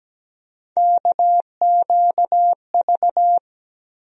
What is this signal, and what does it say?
Morse code, heard as a keyed tone, about 17 wpm, 700 Hz. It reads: KQV
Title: USA Call Sign Postfixes - 3 Letters 17wpm